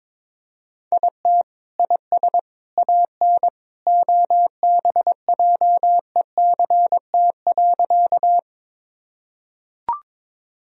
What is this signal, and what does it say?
Morse code, heard as a keyed tone, about 21 wpm, 700 Hz. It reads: IT IS AN OBJECT.
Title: Sentences from Top 1000 Words - Review Part 3 - 22wpm